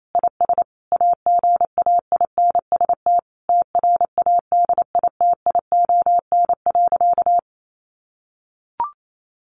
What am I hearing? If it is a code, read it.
Morse code, heard as a keyed tone, about 28 wpm, 700 Hz. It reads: IS AGAINST TRADITION.
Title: Sentences from Top 1500 Words 28wpm